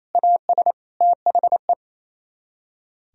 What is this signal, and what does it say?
Morse code, heard as a keyed tone, about 35 wpm, 700 Hz. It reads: AS THE